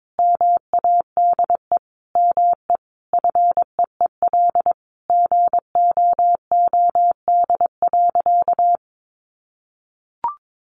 Morse code, heard as a keyed tone, about 22 wpm, 700 Hz. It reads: MADE ME FEEL GOOD.
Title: Sentences from Top 600 Words 22wpm